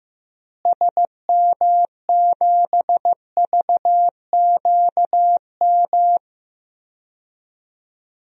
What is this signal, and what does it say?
Morse code, heard as a keyed tone, about 16 wpm, 700 Hz. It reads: SM7VQM
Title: DXCC-100 Call Signs - 6 Characters - Repeated 2x - 15wpm